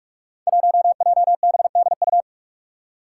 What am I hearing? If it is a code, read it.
Morse code, heard as a keyed tone, about 44 wpm, 700 Hz. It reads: 1JBDU
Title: Five Challenging Letter-Number Combinations 45wpm